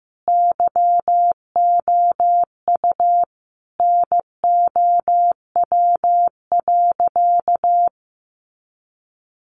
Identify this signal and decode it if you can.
Morse code, heard as a keyed tone, about 15 wpm, 700 Hz. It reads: YOU NOW.
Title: Sentences from Top 1200 Words 15wpm